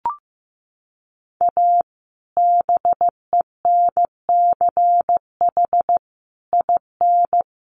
Morse code, heard as a keyed tone, about 15 wpm, 700 Hz. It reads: A BENCH IN
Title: Sentences from Top 1400 Words 15wpm